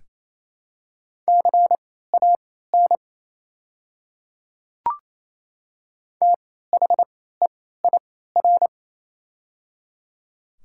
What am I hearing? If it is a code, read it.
Morse code, heard as a keyed tone, about 28 wpm, spaced as for 16 wpm, 700 Hz. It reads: CAN THEIR
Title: 3x Character Spacing - Top 100 Words - 28wpm